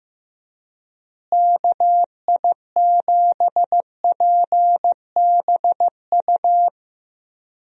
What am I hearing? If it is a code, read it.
Morse code, heard as a keyed tone, about 15 wpm, 700 Hz. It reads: KI7PBU
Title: USA Call Signs - 6 Characters - Repeated 3x - 15wpm